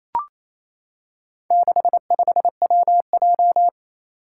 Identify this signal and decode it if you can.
Morse code, heard as a keyed tone, about 27 wpm, 700 Hz. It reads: 65WJ
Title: Four Challenging Letter-Number Combinations 28wpm